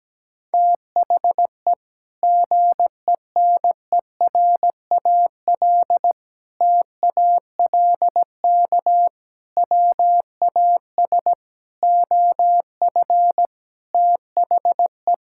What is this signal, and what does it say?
Morse code, heard as a keyed tone, about 17 wpm, 700 Hz. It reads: THE GENERAL TALK WAS OF THE
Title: Sentences from Top 1000 Words - Review Part 5 - 17wpm